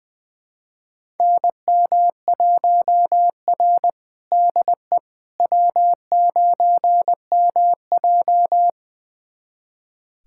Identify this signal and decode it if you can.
Morse code, heard as a keyed tone, about 20 wpm, 700 Hz. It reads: NM1R DE W9MJ
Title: USA Callsign DE Callsign - 4 Characters - 20wpm